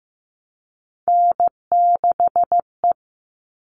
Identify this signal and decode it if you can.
Morse code, heard as a keyed tone, about 15 wpm, 700 Hz. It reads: N6E
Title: All Cut Numbers - 3-Digit 15wpm